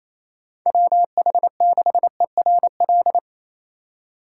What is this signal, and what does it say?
Morse code, heard as a keyed tone, about 35 wpm, 700 Hz. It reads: WH6ERL